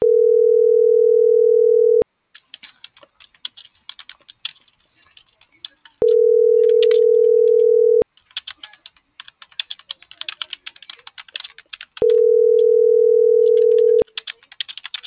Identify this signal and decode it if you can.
Ringback tone (North America), 440 + 480 Hz.